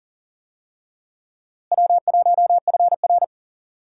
Morse code, heard as a keyed tone, about 39 wpm, 700 Hz. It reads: W1FR
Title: USA Call Signs - 4 Characters - Repeated 3x - 40wpm